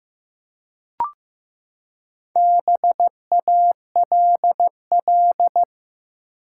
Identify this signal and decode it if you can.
Morse code, heard as a keyed tone, about 15 wpm, 700 Hz. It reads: BALL
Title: Four Letter Words - 15wpm